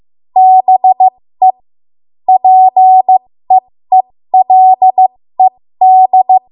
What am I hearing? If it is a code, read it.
Morse code, heard as a keyed tone, about 15 wpm, spaced as for 13 wpm, 750 Hz. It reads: BE PEELED
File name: Morse / 13WPM